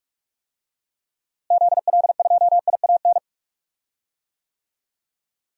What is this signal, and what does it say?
Morse code, heard as a keyed tone, about 41 wpm, 700 Hz. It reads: ZL2IAN
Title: DXCC-100 Call Signs - 6 Characters - Repeated 3x - 45wpm